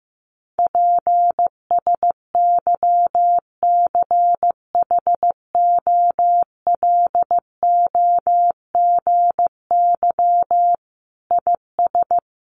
Morse code, heard as a keyed tone, about 15 wpm, 700 Hz. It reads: PSYCHOLOGY IS
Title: Sentences from Top 1400 Words 15wpm